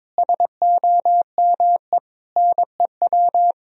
Morse code, heard as a keyed tone, about 22 wpm, 700 Hz. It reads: SOME NEW
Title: Sentences from Top 1000 Words - Review Part 4 - 22wpm